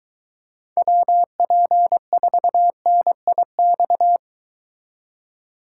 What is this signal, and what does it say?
Morse code, heard as a keyed tone, about 23 wpm, 700 Hz. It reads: WP4NIX